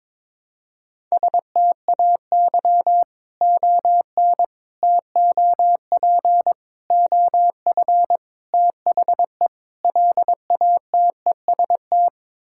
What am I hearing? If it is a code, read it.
Morse code, heard as a keyed tone, about 22 wpm, 700 Hz. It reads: STAY ON TOP OF THE LATEST
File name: practice / 22wpm / Sets of 6 Words - 22wpm